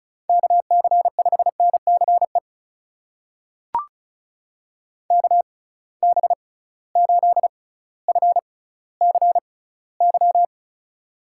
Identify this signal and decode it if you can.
Morse code, heard as a keyed tone, about 36 wpm, spaced as for 18 wpm, 700 Hz. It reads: KC5NCE KB8FCY
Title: USA Call Signs - 6 Characters - 6x Character Spacing - 35wpm